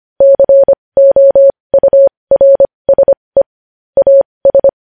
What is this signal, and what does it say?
Morse code, heard as a keyed tone, about 25 wpm, spaced as for 20 wpm, 550 Hz. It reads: COURSE AS
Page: news headlines in morse code at 20 wpm